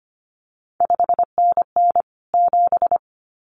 Morse code, heard as a keyed tone, about 25 wpm, 700 Hz. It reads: 5NN 7
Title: CQ WW DX Contest Exchange Encore - 25wpm